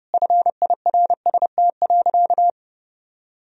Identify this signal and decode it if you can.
Morse code, heard as a keyed tone, about 30 wpm, 700 Hz. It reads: FIRST.
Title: Sentences from Top 1000 Words - Review Part 1 - 30wpm